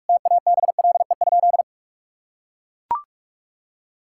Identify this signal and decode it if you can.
Morse code, heard as a keyed tone, about 42 wpm, 700 Hz. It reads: TABLE?
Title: Sentences from Top 1000 Words - Review Part 6 - 45wpm